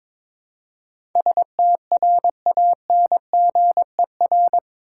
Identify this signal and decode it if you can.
Morse code, heard as a keyed tone, about 22 wpm, 700 Hz. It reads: STRANGER
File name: practice / 22wpm / Eight Letter Words - 22wpm